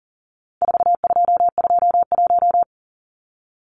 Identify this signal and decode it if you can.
Morse code, heard as a keyed tone, about 39 wpm, 700 Hz. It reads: 4221
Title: Numbers - 4-Digit 40wpm